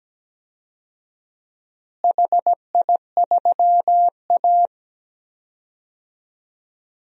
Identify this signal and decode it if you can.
Morse code, heard as a keyed tone, about 17 wpm, 700 Hz. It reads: HI3A
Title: DXCC-100 Call Signs - 4 Characters - Repeated 2x - 17wpm